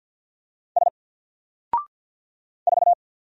Morse code, heard as a keyed tone, about 50 wpm, 700 Hz. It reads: S 4
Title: Single Character (Letter:Number:Punctuation:Prosign) Alt 50wpm